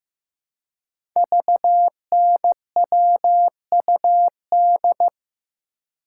Morse code, heard as a keyed tone, about 15 wpm, 700 Hz. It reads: VNWUD
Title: Code Groups - 5 Letters - 15wpm